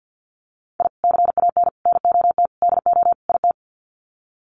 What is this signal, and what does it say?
Morse code, heard as a keyed tone, about 48 wpm, 700 Hz. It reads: I CAN NOT DO IT
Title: Sets of 5 Words Encore 50wpm